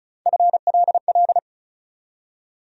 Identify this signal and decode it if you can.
Morse code, heard as a keyed tone, about 34 wpm, 700 Hz. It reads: FLL
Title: USA Call Sign Postfixes - 3 Letters 35wpm